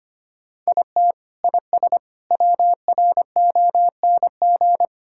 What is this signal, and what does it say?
Morse code, heard as a keyed tone, about 25 wpm, 700 Hz. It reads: IT IS WRONG
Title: Sentences from Top 1000 Words - Review Part 5 - 25wpm